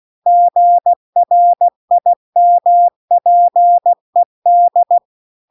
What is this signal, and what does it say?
Morse code, heard as a keyed tone, about 16 wpm, 700 Hz. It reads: GRIMPED